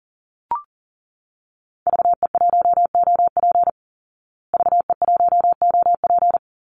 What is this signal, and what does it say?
Morse code, heard as a keyed tone, about 40 wpm, 700 Hz. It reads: VE1OP VE1OP
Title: Call Signs Repeated 3x - 40wpm